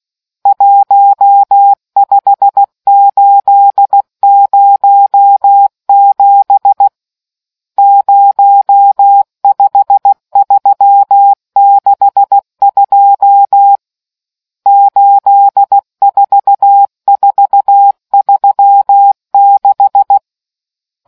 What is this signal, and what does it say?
Morse code, heard as a keyed tone, about 16 wpm, 800 Hz. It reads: 15807 05362 84436